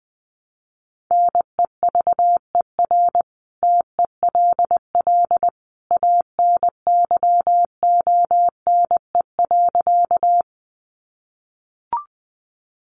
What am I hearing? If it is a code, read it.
Morse code, heard as a keyed tone, about 20 wpm, 700 Hz. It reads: NEVER TELL ANYONE.
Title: Sentences from Top 900 Words 20wpm